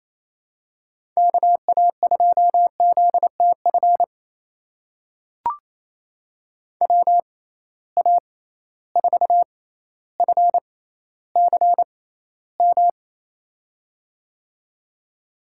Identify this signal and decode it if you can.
Morse code, heard as a keyed tone, about 28 wpm, spaced as for 14 wpm, 700 Hz. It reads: KA2ZTF WA4FCM
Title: USA Call Signs - 6x Character Spacing - 28wpm